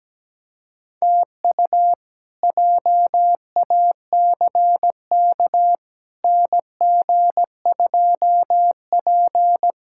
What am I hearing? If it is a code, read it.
Morse code, heard as a keyed tone, about 17 wpm, 700 Hz. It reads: TU JACK NG2P
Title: K1USN SST QSO Practice - 17wpm